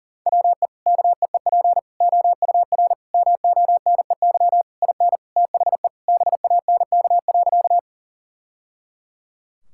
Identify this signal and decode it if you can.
Morse code, heard as a keyed tone, about 40 wpm, 700 Hz. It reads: WE KEEP OUR MONEY IN THE BANK.
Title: Sentences from Top 1000 Words - Review Part 6 - 40wpm